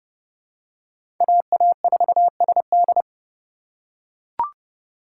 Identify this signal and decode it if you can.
Morse code, heard as a keyed tone, about 30 wpm, 700 Hz. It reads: AA4SD
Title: USA Call Signs - 5 Characters - 4x Character Spacing - 30wpm